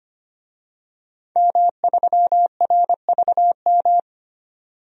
Morse code, heard as a keyed tone, about 25 wpm, 700 Hz. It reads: M3RVM
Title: DXCC-100 Call Signs - 5 Characters - 25wpm